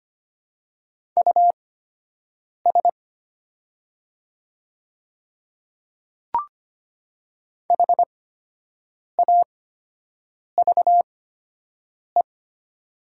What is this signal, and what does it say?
Morse code, heard as a keyed tone, about 25 wpm, spaced as for 7 wpm, 700 Hz. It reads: US HAVE
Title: 8x Character Spacing - Top 500 Words - 25wpm